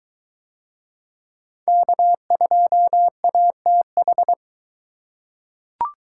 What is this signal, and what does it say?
Morse code, heard as a keyed tone, about 22 wpm, 700 Hz. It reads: K2ATH